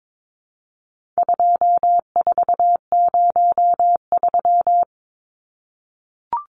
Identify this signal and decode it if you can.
Morse code, heard as a keyed tone, about 22 wpm, 700 Hz. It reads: 2403
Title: Numbers - 4-Digit 22wpm